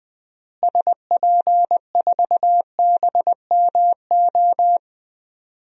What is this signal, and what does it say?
Morse code, heard as a keyed tone, about 20 wpm, 700 Hz. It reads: SP4BMO
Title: DXCC-100 Call Signs - 6 Characters - Repeated 3x - 20wpm